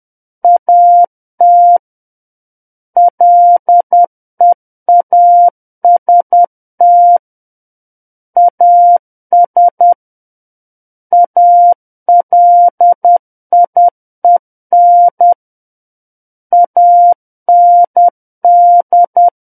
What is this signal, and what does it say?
Morse code, heard as a keyed tone, about 10 wpm, 700 Hz. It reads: AT LEAST AS ALIEN AND
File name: Book One / 01-Chapter 1 - 10 WPM